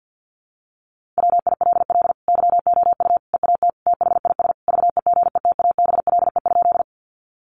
Morse code, heard as a keyed tone, about 50 wpm, 700 Hz. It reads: WILL YOU EAT THIS VEGETABLE?
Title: Sentences from Top 1300 Words 50wpm